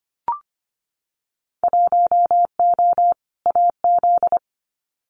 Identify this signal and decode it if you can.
Morse code, heard as a keyed tone, about 25 wpm, 700 Hz. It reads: 1O AZ
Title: Winter Field Day Exchanges 25wpm